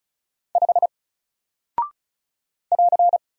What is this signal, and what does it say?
Morse code, heard as a keyed tone, about 34 wpm, 700 Hz. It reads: 5 <AR>
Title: Single Character (Letter:Number:Punctuation:Prosign) 35wpm